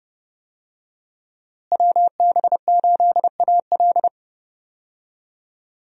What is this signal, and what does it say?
Morse code, heard as a keyed tone, about 30 wpm, 700 Hz. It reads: WB8AL
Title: USA Call Signs - 5 Characters - Repeated 2x - 30wpm